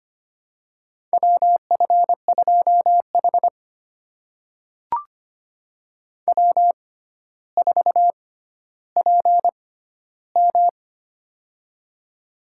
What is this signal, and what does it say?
Morse code, heard as a keyed tone, about 25 wpm, spaced as for 13 wpm, 700 Hz. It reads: WF2H W4PM
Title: USA Call Signs - 4 Characters - 6x Character Spacing - 25wpm